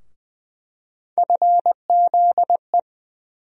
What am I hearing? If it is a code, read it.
Morse code, heard as a keyed tone, about 20 wpm, 700 Hz. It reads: FZE
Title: USA Call Sign Postfixes - 3 Letters 20wpm